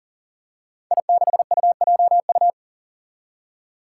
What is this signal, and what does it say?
Morse code, heard as a keyed tone, about 42 wpm, 700 Hz. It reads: I6UJU
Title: Five Challenging Letter-Number Combinations 40wpm